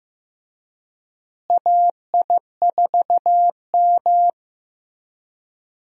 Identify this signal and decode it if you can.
Morse code, heard as a keyed tone, about 15 wpm, 700 Hz. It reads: AI4M
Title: USA Call Signs - 4 Characters - Repeated 3x - 15wpm